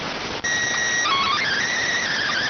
SSTV (slow-scan television), PD 290.